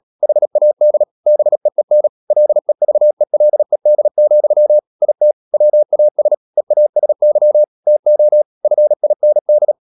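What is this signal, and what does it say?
Morse code, heard as a keyed tone, about 37 wpm, 600 Hz. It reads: HAD BEEN LEVELED, IT WAS EASY TO FIND
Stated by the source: Morse code practice at 35–39 wpm sending English text of Ken Follett's novel Never, from CW med Gnister.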